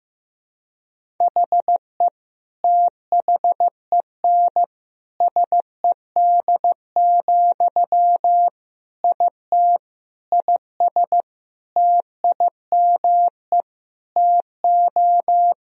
Morse code, heard as a keyed tone, about 15 wpm, 700 Hz. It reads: HE THEN SED, IT IS TIME TO
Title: Sentences from Top 300 Words and Abbreviations 15wpm